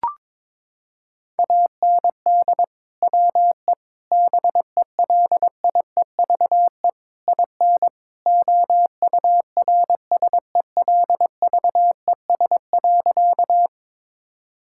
Morse code, heard as a keyed tone, about 22 wpm, 700 Hz. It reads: AND WE BELIEVE IN OURSELVES.
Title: Sentences from Top 1600 Words 22wpm